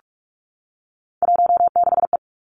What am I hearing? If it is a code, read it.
Morse code, heard as a keyed tone, about 46 wpm, 700 Hz. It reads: 16E
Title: Three Challenging Letter-Number Combinations 45wpm